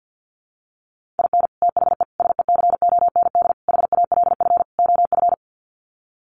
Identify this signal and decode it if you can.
Morse code, heard as a keyed tone, about 50 wpm, 700 Hz. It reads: IN THE SECOND HALF OF
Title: Sets of 5 Words 50wpm